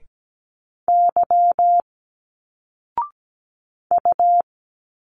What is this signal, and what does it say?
Morse code, heard as a keyed tone, about 17 wpm, 700 Hz. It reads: Y U
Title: Single Letters 17wpm